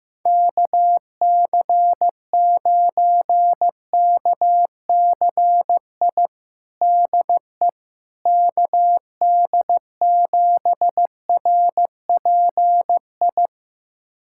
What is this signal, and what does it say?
Morse code, heard as a keyed tone, about 15 wpm, 700 Hz. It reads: KC9KCI DE KD7RPI
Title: USA Callsign DE Callsign 15wpm